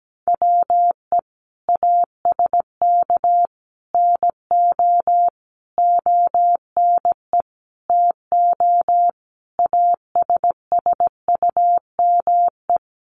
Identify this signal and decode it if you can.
Morse code, heard as a keyed tone, about 17 wpm, 700 Hz. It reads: WE ASK NO ONE TO ASSUME
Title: Sentences from Top 1500 Words 17wpm